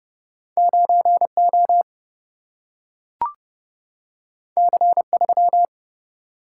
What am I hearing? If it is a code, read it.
Morse code, heard as a keyed tone, about 30 wpm, 700 Hz. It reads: 9O C3
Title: Single-Letter Single-Number 30wpm